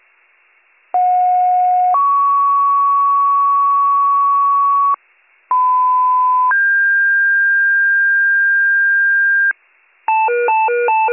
Two-tone page, heard as a one-second tone, then a three-second tone, about 730 then 1120 Hz, about 1010 then 1600 Hz. It is heard from 0.9 to 9.5 s.